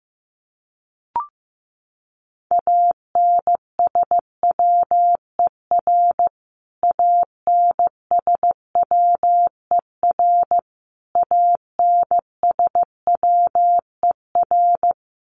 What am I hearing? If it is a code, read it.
Morse code, heard as a keyed tone, about 15 wpm, 700 Hz. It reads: ANSWER ANSWER ANSWER
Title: Top 500 Words Repeated 15wpm